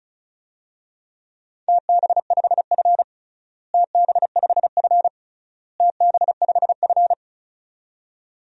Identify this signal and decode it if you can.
Morse code, heard as a keyed tone, about 34 wpm, 700 Hz. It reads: TB5F TB5F TB5F
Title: DXCC-100 Call Signs - 4 Characters - Repeated 3x - 35wpm